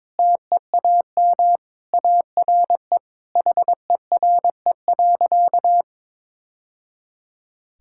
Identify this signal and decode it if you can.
Morse code, heard as a keyed tone, about 22 wpm, 700 Hz. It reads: TEAM ARE HERE.
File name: practice / 22wpm / Sentences with 3-Word Phrases - 22wpm